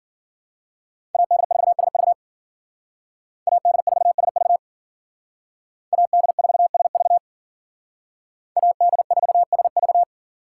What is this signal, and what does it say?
Morse code, heard as a keyed tone, about 52 wpm, 700 Hz. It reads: AD4SV AD4SV AD4SV AD4SV